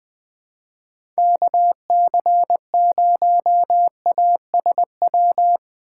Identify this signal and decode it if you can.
Morse code, heard as a keyed tone, about 20 wpm, 700 Hz. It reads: KC0ASW
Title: USA Call Signs - 2x Character Spacing - 20wpm